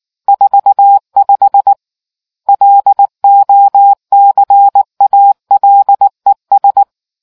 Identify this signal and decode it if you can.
Morse code, heard as a keyed tone, about 18 wpm, 800 Hz. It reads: 45 LOCALES